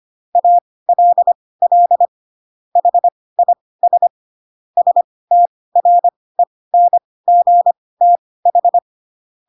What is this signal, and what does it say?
Morse code, heard as a keyed tone, about 25 wpm, spaced as for 18 wpm, 700 Hz. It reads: ALL HIS STRENGTH
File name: Pinocchio / Pinocchio ch4 12 25 ik1hge